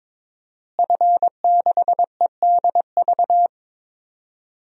Morse code, heard as a keyed tone, about 22 wpm, 700 Hz. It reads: F6EDV
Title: DXCC-100 Call Signs - 5 Characters - Repeated 2x - 22wpm